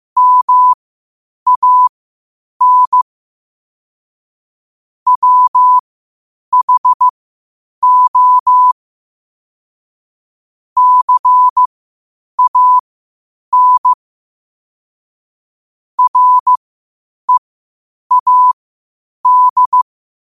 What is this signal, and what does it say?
Morse code, heard as a keyed tone, about 15 wpm, spaced as for 8 wpm, 1000 Hz. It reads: MAN WHO CAN READ
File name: 8wpm